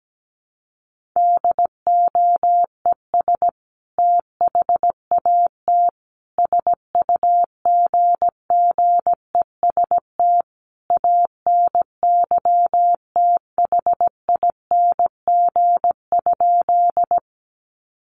Morse code, heard as a keyed tone, about 17 wpm, 700 Hz. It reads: DOES THAT SUGGEST ANYTHING?
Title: Sentences from Top 1300 Words 17wpm